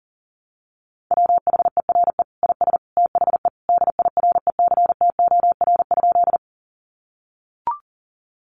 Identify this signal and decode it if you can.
Morse code, heard as a keyed tone, about 40 wpm, 700 Hz. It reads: WHERE IS THE DIRECTOR?
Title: Sentences from Top 1300 Words 40wpm